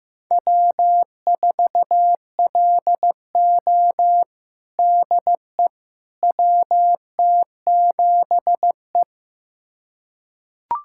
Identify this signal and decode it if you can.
Morse code, heard as a keyed tone, about 15 wpm, 700 Hz. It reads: W4LO DE WT7E
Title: USA Callsign DE Callsign - 4 Characters - 15wpm